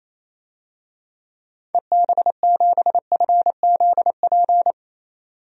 Morse code, heard as a keyed tone, about 28 wpm, 700 Hz. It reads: EB7FZP
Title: DXCC-100 Call Signs - 6 Characters - 28wpm